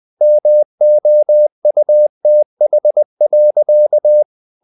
Morse code, heard as a keyed tone, about 20 wpm, 600 Hz. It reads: MOUTH.
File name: the old man and the sea / the old man and the sea part 1 20wpm